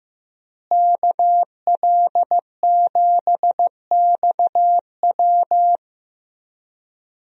Morse code, heard as a keyed tone, about 16 wpm, 700 Hz. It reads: KL7XW
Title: DXCC-100 Call Signs - 5 Characters - Repeated 2x - 15wpm